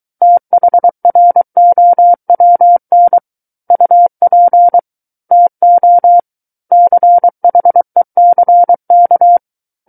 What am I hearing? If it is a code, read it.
Morse code, heard as a keyed tone, about 23 wpm, 700 Hz. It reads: THROWN UP TO CHECK
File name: Book One / 14-Chapter 14 - 23 WPM